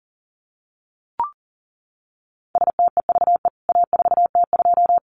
Morse code, heard as a keyed tone, about 40 wpm, 700 Hz. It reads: STEVE A4T2
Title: CWT Exchange 40wpm